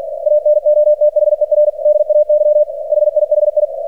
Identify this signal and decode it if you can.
Morse code, heard as a keyed tone, about 47 wpm, 600 Hz, with radio noise. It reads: ANOTHER DAY HIHI
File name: archived / CRSnet